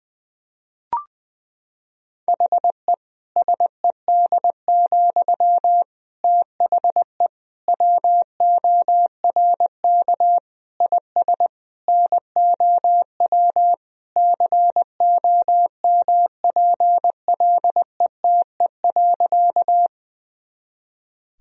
Morse code, heard as a keyed tone, about 21 wpm, 700 Hz. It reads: HE SED, THE WORK IS NOW COMPLETE.
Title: Sentences from Top 400 Words and Abbreviations 20wpm